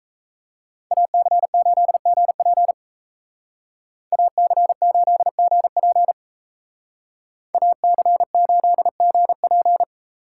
Morse code, heard as a keyed tone, about 36 wpm, 700 Hz. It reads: AC8GP AC8GP AC8GP